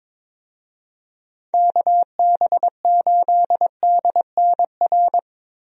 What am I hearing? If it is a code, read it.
Morse code, heard as a keyed tone, about 22 wpm, 700 Hz. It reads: KB8DNR
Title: USA Call Signs - 6 Characters - Repeated 2x - 22wpm